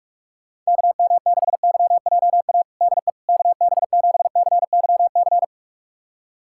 Morse code, heard as a keyed tone, about 46 wpm, 700 Hz. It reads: KM6YJA DE KB7CYC
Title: USA Callsign DE Callsign - 6 Characters - 45wpm